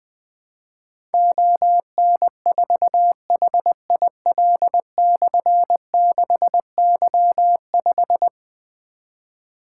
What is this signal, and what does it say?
Morse code, heard as a keyed tone, about 20 wpm, 700 Hz. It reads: ON4HIL/6Y5
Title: Difficult Call Signs 20wpm